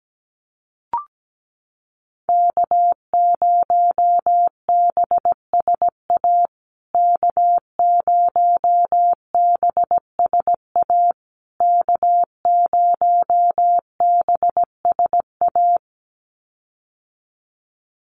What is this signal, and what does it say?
Morse code, heard as a keyed tone, about 17 wpm, 700 Hz. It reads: K0BSA K0BSA K0BSA
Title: USA Call Signs Repeated 3x 17wpm